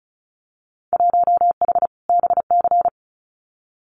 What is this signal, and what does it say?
Morse code, heard as a keyed tone, about 36 wpm, 700 Hz. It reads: 1H BC
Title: Winter Field Day Exchanges 35wpm